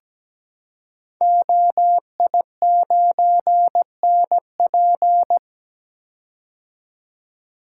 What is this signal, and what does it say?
Morse code, heard as a keyed tone, about 17 wpm, 700 Hz. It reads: OI9NP
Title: Code Groups - 5 Letter:Numbers - 17wpm